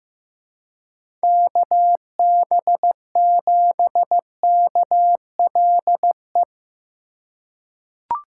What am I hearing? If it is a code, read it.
Morse code, heard as a keyed tone, about 15 wpm, 700 Hz. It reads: KB7KLE
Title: USA Call Signs - 6 Characters - 4x Character Spacing - 15wpm